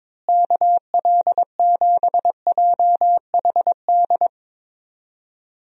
Morse code, heard as a keyed tone, about 22 wpm, 700 Hz. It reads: KL7JHD
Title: USA Call Signs - 6 Characters - Repeated 3x - 22wpm